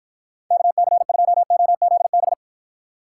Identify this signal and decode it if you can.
Morse code, heard as a keyed tone, about 53 wpm, 700 Hz. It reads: KC2OZB